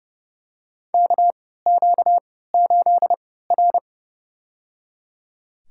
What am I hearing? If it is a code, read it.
Morse code, heard as a keyed tone, about 30 wpm, spaced as for 17 wpm, 700 Hz. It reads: KQ8R
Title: USA Call Signs - 4 Characters - 3x Character Spacing - 30wpm